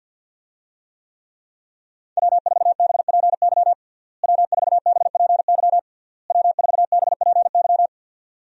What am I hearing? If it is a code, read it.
Morse code, heard as a keyed tone, about 50 wpm, 700 Hz. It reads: W4BPY W4BPY W4BPY
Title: USA Call Signs - 5 Characters - Repeated 3x - 50wpm